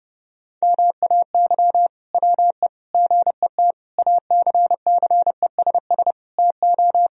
Morse code, heard as a keyed tone, about 30 wpm, 700 Hz. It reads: MAY WE GET ACCESS TO
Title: Sentences from Top 1000 Words - Review Part 4 - 30wpm